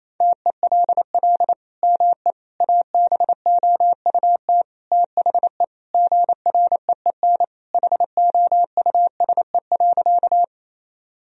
Morse code, heard as a keyed tone, about 28 wpm, 700 Hz. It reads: TELL ME ABOUT THE GREEN HOUSE.
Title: Sentences from Top 1000 Words - Review Part 2 - 28wpm